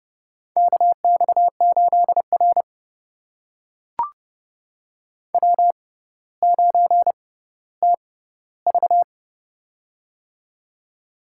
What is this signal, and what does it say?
Morse code, heard as a keyed tone, about 30 wpm, spaced as for 15 wpm, 700 Hz. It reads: KX8R W9TV
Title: USA Call Signs - 4 Characters - 6x Character Spacing - 30wpm